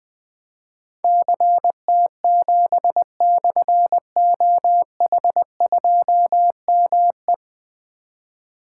Morse code, heard as a keyed tone, about 20 wpm, 700 Hz. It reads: CT7/OH2ME
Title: Difficult Call Signs 20wpm